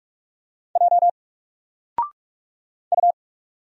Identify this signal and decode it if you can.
Morse code, heard as a keyed tone, about 46 wpm, 700 Hz. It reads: J U
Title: Single Letter-Number 45wpm